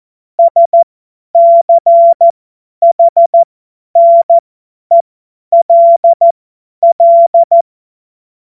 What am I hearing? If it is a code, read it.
Morse code, heard as a keyed tone, about 14 wpm, spaced as for 10 wpm, 680 Hz. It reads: SCHNELL
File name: Morsekurs / Witze / Witz 15. August 2023 50 F70